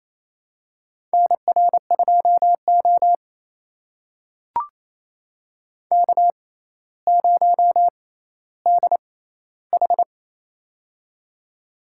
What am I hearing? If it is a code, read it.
Morse code, heard as a keyed tone, about 28 wpm, spaced as for 14 wpm, 700 Hz. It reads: NR2O K0DH
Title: USA Call Signs - 4 Characters - 6x Character Spacing - 28wpm